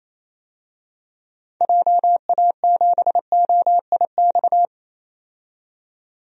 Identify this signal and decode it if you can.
Morse code, heard as a keyed tone, about 28 wpm, 700 Hz. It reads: JA7OIX
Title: DXCC-100 Call Signs - 6 Characters - 28wpm